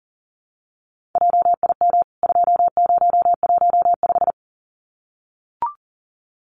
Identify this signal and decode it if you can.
Morse code, heard as a keyed tone, about 40 wpm, 700 Hz. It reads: JIM 2015
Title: CWT Exchange 40wpm